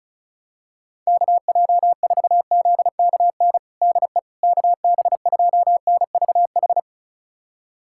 Morse code, heard as a keyed tone, about 36 wpm, 700 Hz. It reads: KJ4ZKN DE KB2NVH
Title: USA Callsign DE Callsign - 6 Characters - 35wpm